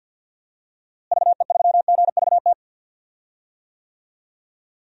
Morse code, heard as a keyed tone, about 51 wpm, 700 Hz. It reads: VE3GVT
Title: DXCC-100 Call Signs - 6 Characters - Repeated 3x - 50wpm